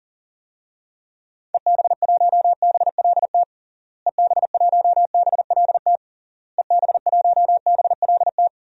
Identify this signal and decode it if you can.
Morse code, heard as a keyed tone, about 40 wpm, 700 Hz. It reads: EB1BLT EB1BLT EB1BLT
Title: DXCC-100 Call Signs - 6 Characters - Repeated 3x - 40wpm